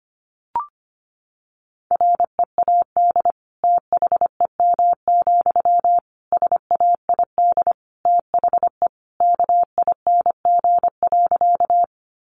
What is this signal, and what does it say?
Morse code, heard as a keyed tone, about 25 wpm, 700 Hz. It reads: READ THEM, SAID THE KING.
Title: Sentences from Top 400 Words 25wpm